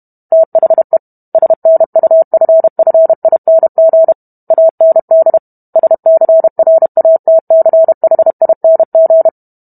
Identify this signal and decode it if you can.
Morse code, heard as a keyed tone, about 32 wpm, 650 Hz. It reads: THE SNUFFING AND SCRATCHING